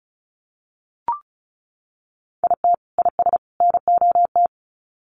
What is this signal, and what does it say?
Morse code, heard as a keyed tone, about 34 wpm, 700 Hz. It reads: IT IS NOT
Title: Sets of 3 Words 35wpm